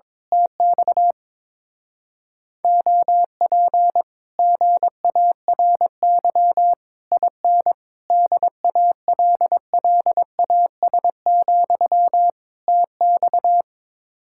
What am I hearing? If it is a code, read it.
Morse code, heard as a keyed tone, about 22 wpm, 700 Hz. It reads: TX OP GARY IN DALLAS, TX
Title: Speed-Racing Common QSO Phrases Encore - 22wpm